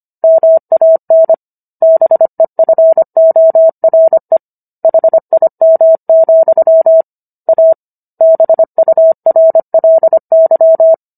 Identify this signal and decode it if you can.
Morse code, heard as a keyed tone, about 25 wpm, 650 Hz. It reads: MAN BEFORE HIM, A BURLY